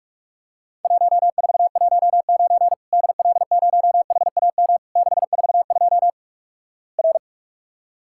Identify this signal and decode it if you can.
Morse code, heard as a keyed tone, about 46 wpm, 700 Hz. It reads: 1419 DL0HAM 642 R
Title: Worked All Europe DX Contest - QTC Exchange - 45wpm